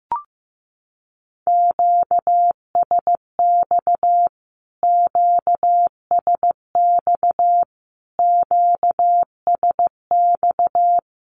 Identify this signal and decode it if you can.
Morse code, heard as a keyed tone, about 15 wpm, 700 Hz. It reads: QSX QSX QSX
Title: Q-Codes Repeated 3x 15wpm